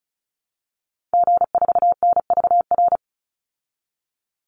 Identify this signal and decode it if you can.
Morse code, heard as a keyed tone, about 34 wpm, 700 Hz. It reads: G4NVR
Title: Call Signs Repeated 3x - 35wpm